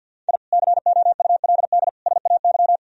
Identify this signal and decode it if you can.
Morse code, heard as a keyed tone, about 47 wpm, 700 Hz. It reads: I COULD SAY
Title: Sentences from Top 1000 Words - Review Part 2 - 50wpm